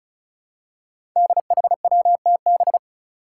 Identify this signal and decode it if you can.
Morse code, heard as a keyed tone, about 36 wpm, 700 Hz. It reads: DHWTB
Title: Five Challenging Letter-Number Combinations 35wpm